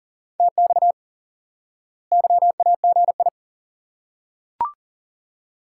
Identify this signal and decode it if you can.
Morse code, heard as a keyed tone, about 40 wpm, 700 Hz. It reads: TX YAGI
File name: IQE Course 3 - Lesson 89 - YAGI - Two Elements - 40wpm